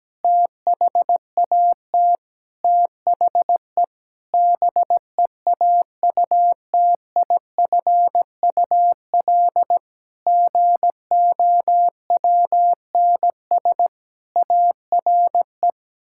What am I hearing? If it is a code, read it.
Morse code, heard as a keyed tone, about 17 wpm, 700 Hz. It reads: THAT THE BEAUTIFUL GOWNS ARE
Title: Pangram Challenge v2 - 17wpm